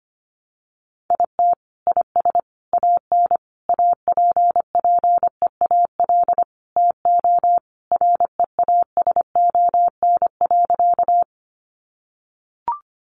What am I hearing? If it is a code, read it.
Morse code, heard as a keyed tone, about 25 wpm, 700 Hz. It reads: IT IS AN APPEAL TO REASON.
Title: Sentences from Top 1300 Words 25wpm